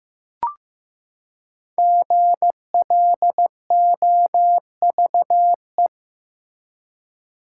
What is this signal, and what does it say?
Morse code, heard as a keyed tone, about 15 wpm, 700 Hz. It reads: GLOVE
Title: Five Letter Words - 15wpm